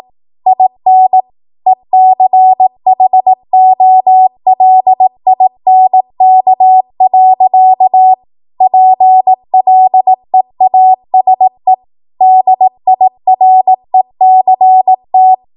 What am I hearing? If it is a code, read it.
Morse code, heard as a keyed tone, about 18 wpm, 750 Hz. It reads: IN ECHOLINK. PLEASE DIRECT
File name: Archive / 18 WPM / 18WPM